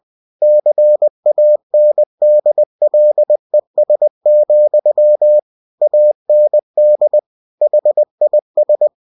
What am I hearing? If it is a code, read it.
Morse code, heard as a keyed tone, about 20 wpm, 600 Hz. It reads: CANDLES, AND HIS